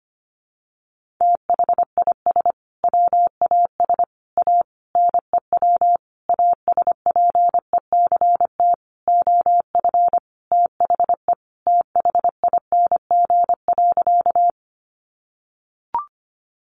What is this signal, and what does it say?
Morse code, heard as a keyed tone, about 25 wpm, 700 Hz. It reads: THIS WAS A NEW ASPECT OF THE THING.
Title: Sentences from Top 1300 Words 25wpm